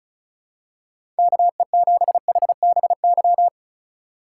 Morse code, heard as a keyed tone, about 36 wpm, 700 Hz. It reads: KE7HBY
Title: USA Call Signs - 6 Characters - 3x Character Spacing - 35wpm